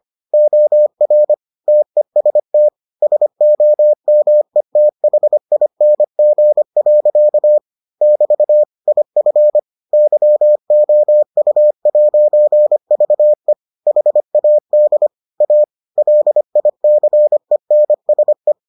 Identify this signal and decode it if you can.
Morse code, heard as a keyed tone, about 25 wpm, 600 Hz. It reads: OR TEST SOMETHING. <BT> IF YOU'VE HAD A LICENSE